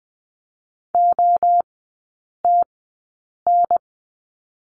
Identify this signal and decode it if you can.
Morse code, heard as a keyed tone, about 20 wpm, spaced as for 8 wpm, 700 Hz. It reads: OTN
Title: 047 Lesson 9e - Number 4 - Three Characters 20wpm